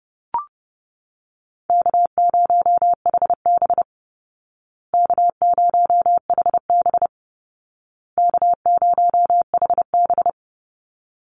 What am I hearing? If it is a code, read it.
Morse code, heard as a keyed tone, about 30 wpm, 700 Hz. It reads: K0HB K0HB K0HB
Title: Call Signs Repeated 2x - 30wpm